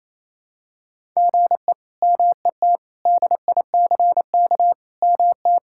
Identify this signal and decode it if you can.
Morse code, heard as a keyed tone, about 28 wpm, 700 Hz. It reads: GE MET DICK MT
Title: K1USN SST QSO Practice - 28wpm